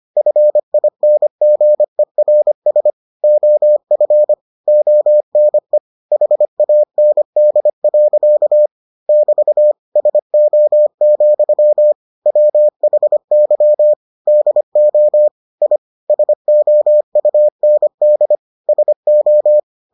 Morse code, heard as a keyed tone, about 25 wpm, 600 Hz. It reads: FINGERS OF ONE HAND. <BT> SO, WHY DO I SOUND SO